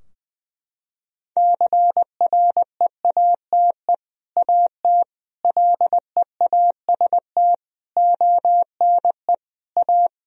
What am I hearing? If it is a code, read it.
Morse code, heard as a keyed tone, about 20 wpm, 700 Hz. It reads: CREATE AT LEAST ONE A
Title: Sentences from Top 1000 Words - Review Part 3 - 20wpm